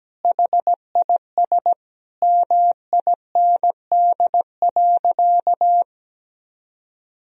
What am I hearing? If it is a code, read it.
Morse code, heard as a keyed tone, about 17 wpm, 700 Hz. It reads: HIS MIND.
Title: Sentences from Top 1000 Words - Review Part 2 - 17wpm